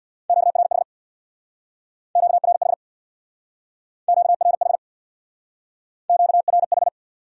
Morse code, heard as a keyed tone, about 64 wpm, 700 Hz. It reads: QRH QRH QRH QRH